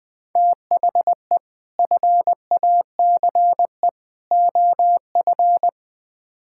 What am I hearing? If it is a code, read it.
Morse code, heard as a keyed tone, about 21 wpm, 700 Hz. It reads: THE FACE OF
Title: Build A Phrase - 3-Word Phrases - 20wpm